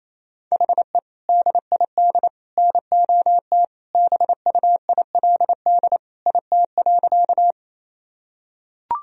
Morse code, heard as a keyed tone, about 27 wpm, 700 Hz. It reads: HE DID NOT BUILD IT.
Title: Sentences from Top 1000 Words - Review Part 1 - 28wpm